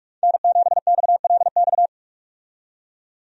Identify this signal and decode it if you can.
Morse code, heard as a keyed tone, about 46 wpm, 700 Hz. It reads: N7XLX
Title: USA Call Signs - 5 Characters - 45wpm